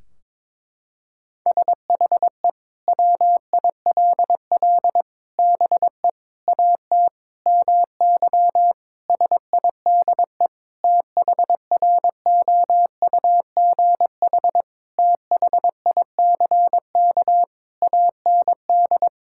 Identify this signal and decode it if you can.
Morse code, heard as a keyed tone, about 22 wpm, 700 Hz. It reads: SHE WILL BE AT MY SIDE THROUGH THICK AND
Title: Sentences from Top 1600 Words 22wpm